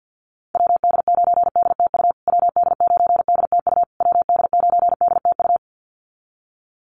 Morse code, heard as a keyed tone, about 50 wpm, 700 Hz. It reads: WD9DTU WD9DTU WD9DTU
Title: USA Call Signs Repeated 3x 50wpm